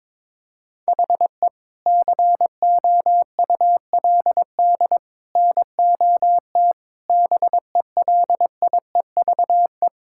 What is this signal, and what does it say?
Morse code, heard as a keyed tone, about 22 wpm, 700 Hz. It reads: HE COULD NOT BELIEVE
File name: practice / 22wpm / Sentences with 3-Word Phrases - 22wpm